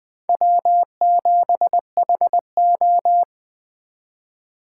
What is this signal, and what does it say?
Morse code, heard as a keyed tone, about 20 wpm, 700 Hz. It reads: W7HO
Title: 2023 SOTA Chaser Call Signs - 20wpm